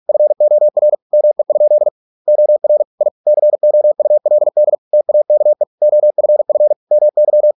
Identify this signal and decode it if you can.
Morse code, heard as a keyed tone, about 46 wpm, 600 Hz. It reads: FOR ME? OR I COULD TAKE OFF MY